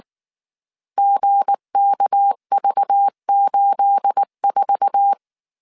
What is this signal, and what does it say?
Morse code, heard as a keyed tone, about 19 wpm, 800 Hz. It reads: GKV84
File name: CW / DET / 1 9 1500C S A 18wpm 2459